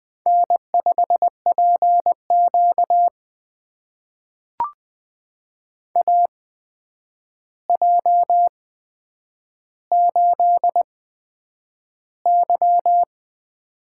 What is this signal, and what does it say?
Morse code, heard as a keyed tone, about 20 wpm, spaced as for 9 wpm, 700 Hz. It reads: N5PQ AJ8Y
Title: USA Call Signs - 4 Characters - 8x Character Spacing - 20wpm